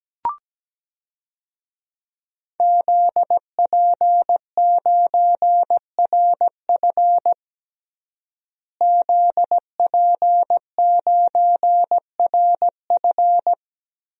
Morse code, heard as a keyed tone, about 17 wpm, 700 Hz. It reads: ZP9RF ZP9RF
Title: DXCC-100 Call Signs - 5 Characters - Repeated 3x - 17wpm